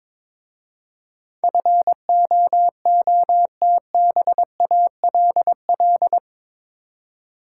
Morse code, heard as a keyed tone, about 22 wpm, 700 Hz. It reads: FOOTBALL
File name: Difficult Words with Hints - 22wpm